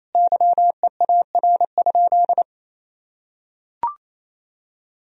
Morse code, heard as a keyed tone, about 28 wpm, 700 Hz. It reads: YEAR?
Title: Sentences from Top 1000 Words - Review Part 4 - 28wpm